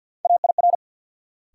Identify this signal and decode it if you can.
Morse code, heard as a keyed tone, about 53 wpm, 700 Hz. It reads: AIR
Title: Sets of 2 Words - New Word Drill - 50wpm